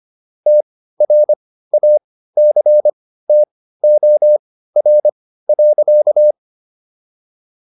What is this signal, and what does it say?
Morse code, heard as a keyed tone, about 25 wpm, spaced as for 15 wpm, 600 Hz. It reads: TRACTOR.